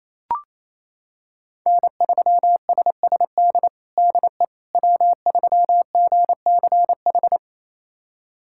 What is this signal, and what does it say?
Morse code, heard as a keyed tone, about 28 wpm, 700 Hz. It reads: N3SSD DE W3GCH
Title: USA Callsign DE Callsign - 5 Characters - 28wpm